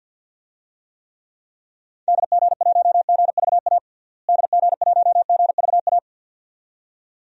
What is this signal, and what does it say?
Morse code, heard as a keyed tone, about 50 wpm, 700 Hz. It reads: DG1GVA DG1GVA
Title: DXCC-100 Call Signs - 6 Characters - Repeated 2x - 50wpm